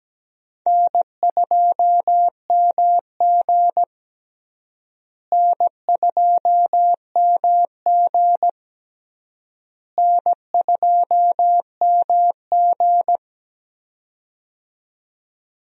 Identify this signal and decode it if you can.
Morse code, heard as a keyed tone, about 17 wpm, 700 Hz. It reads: N2MG N2MG N2MG
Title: USA Call Signs - 4 Characters - Repeated 3x - 17wpm